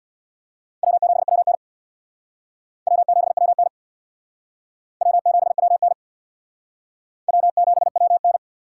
Morse code, heard as a keyed tone, about 60 wpm, 700 Hz. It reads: W7WN W7WN W7WN W7WN
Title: Speed Racing - USA 1x2 Call Signs 50wpm